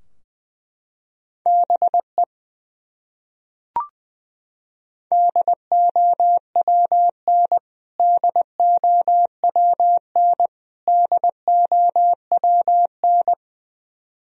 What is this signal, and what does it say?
Morse code, heard as a keyed tone, about 20 wpm, 700 Hz. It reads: BE DOWN DOWN DOWN
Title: Top 100 Words Repeated 20wpm